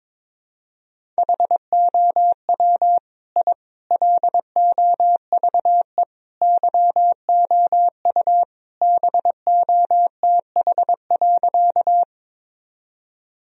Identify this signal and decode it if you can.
Morse code, heard as a keyed tone, about 22 wpm, 700 Hz. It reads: HOW I LOVE YOU BOTH.
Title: Sentences from Top 1000 Words - Review Part 1 - 22wpm